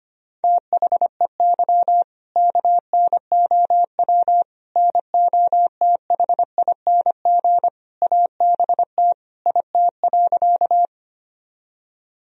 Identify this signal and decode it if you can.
Morse code, heard as a keyed tone, about 25 wpm, 700 Hz. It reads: THEY KNOW NOTHING ABT IT.
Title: Sentences from Top 500 Words and Abbreviations 25wpm